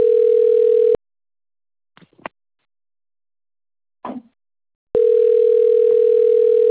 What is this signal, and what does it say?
Ringback tone (North America), 440 + 480 Hz.